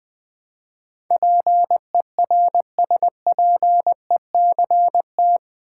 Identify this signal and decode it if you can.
Morse code, heard as a keyed tone, about 21 wpm, 700 Hz. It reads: PERSPECT
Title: Build A Word - 4 Letters At A Time - 11-Letter Words - 20wpm